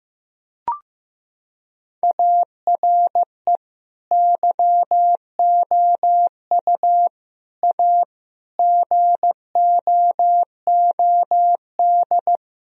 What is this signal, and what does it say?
Morse code, heard as a keyed tone, about 16 wpm, 700 Hz. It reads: ARE YOU A GOOD
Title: Sentences from Top 1000 Words - Review Part 4 - 15wpm